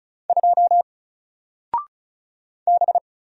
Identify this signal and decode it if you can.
Morse code, heard as a keyed tone, about 36 wpm, 700 Hz. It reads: 2 B
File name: Single-Letter-Number-35wpm